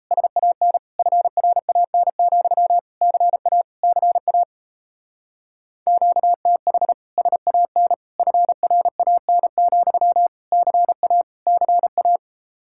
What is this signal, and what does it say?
Morse code, heard as a keyed tone, about 35 wpm, 700 Hz. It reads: SAN FRAN, CA CA QTH SAN FRAN, CA CA
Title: Speed-Racing Common QSO Phrases Encore - 28wpm